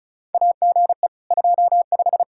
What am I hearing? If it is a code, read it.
Morse code, heard as a keyed tone, about 36 wpm, 700 Hz. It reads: AGE 25
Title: Practice QSO - Exchange 3 - More Info - 35wpm